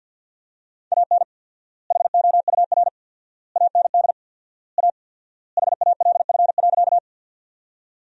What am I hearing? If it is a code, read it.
Morse code, heard as a keyed tone, about 50 wpm, 700 Hz. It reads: AN HOUR AND A HALF.
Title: 4x Word-Spacing - Sentences from Top 400 Words - 50wpm